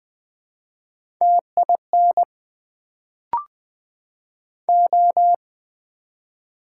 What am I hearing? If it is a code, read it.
Morse code, heard as a keyed tone, about 20 wpm, 700 Hz. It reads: TIN O
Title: Build A Word - 1 Letter At A Time - 3-Letter Words - 20wpm